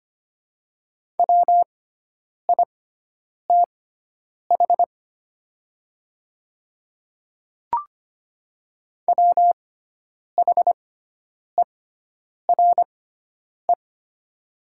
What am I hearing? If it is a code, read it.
Morse code, heard as a keyed tone, about 25 wpm, spaced as for 9 wpm, 700 Hz. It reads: WITH WHERE
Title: 6x Character Spacing - Top 300 Words - 25wpm